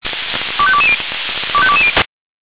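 Five-tone selective call (ZVEI), heard as quick runs of five tones, tones of about 70 ms. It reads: 25220 25220
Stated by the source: ZVEI five-tone call received on SDR.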